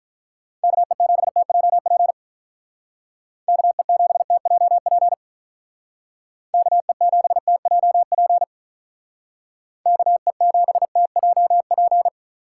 Morse code, heard as a keyed tone, about 46 wpm, 700 Hz. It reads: KE7TJP KE7TJP KE7TJP KE7TJP